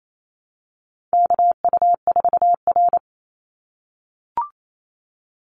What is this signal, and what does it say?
Morse code, heard as a keyed tone, about 28 wpm, 700 Hz. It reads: KU4R
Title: NA SOTA Chaser Callsigns 28wpm